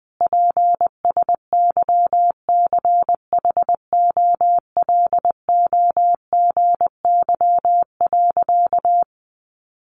Morse code, heard as a keyed tone, about 20 wpm, 700 Hz. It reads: PSYCHOLOGY.
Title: Sentences from Top 1400 Words 20wpm